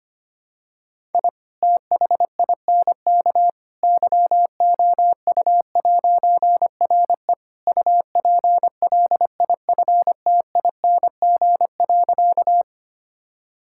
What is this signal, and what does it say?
Morse code, heard as a keyed tone, about 25 wpm, 700 Hz. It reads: I THINK YOU'RE UPLIFTING.
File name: practice / 25wpm / All-Age Appropriate Jokes - 25wpm